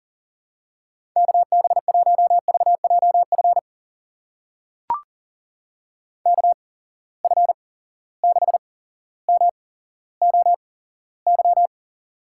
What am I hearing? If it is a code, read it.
Morse code, heard as a keyed tone, about 40 wpm, spaced as for 17 wpm, 700 Hz. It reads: KB1VJF KF6MOY
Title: USA Call Signs - 6 Characters - 8x Character Spacing - 40wpm